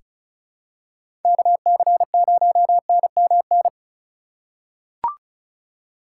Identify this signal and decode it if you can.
Morse code, heard as a keyed tone, about 34 wpm, 700 Hz. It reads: KC0NMN
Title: USA Call Signs - 6 Characters - 4x Character Spacing - 35wpm